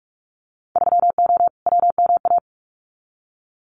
Morse code, heard as a keyed tone, about 46 wpm, 700 Hz. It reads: 3O WMA
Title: Winter Field Day Exchanges Encore 45wpm